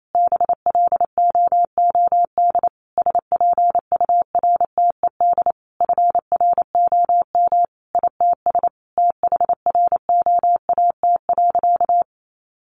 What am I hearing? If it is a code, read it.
Morse code, heard as a keyed tone, about 28 wpm, 700 Hz. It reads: BLOOD SPURTED FROM ITS THROAT.
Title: A Sound Of Thunder 28wpm